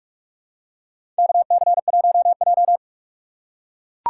Morse code, heard as a keyed tone, about 44 wpm, 700 Hz. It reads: KC1J
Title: USA Call Signs - 4 Characters - 6x Character Spacing - 45wpm